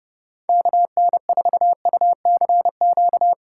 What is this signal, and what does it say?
Morse code, heard as a keyed tone, about 31 wpm, 700 Hz. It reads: KN4UCQ
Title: USA Callsign DE Callsign - 6 Characters - 30wpm